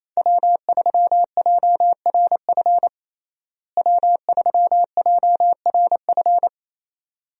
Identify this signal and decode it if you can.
Morse code, heard as a keyed tone, about 28 wpm, 700 Hz. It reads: W3JRF W3JRF
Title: USA Call Signs - 5 Characters - Repeated 2x - 28wpm